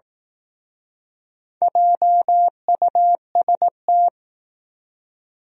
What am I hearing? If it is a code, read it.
Morse code, heard as a keyed tone, about 19 wpm, 700 Hz. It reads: JUST
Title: Speed-Racing Top 200 Words 15wpm